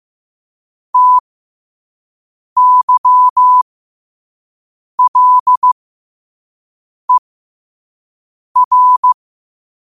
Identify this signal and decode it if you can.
Morse code, heard as a keyed tone, about 15 wpm, spaced as for 5 wpm, 1000 Hz. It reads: TYLER
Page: qotd-at-5-wpm